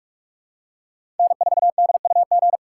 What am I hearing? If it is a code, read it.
Morse code, heard as a keyed tone, about 43 wpm, 700 Hz. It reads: N4DUG
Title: USA Call Signs - 5 Characters - 3x Character Spacing - 45wpm